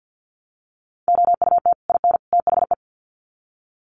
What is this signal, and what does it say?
Morse code, heard as a keyed tone, about 50 wpm, 700 Hz. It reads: OUT IN THE
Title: Sets of 3 Words 50wpm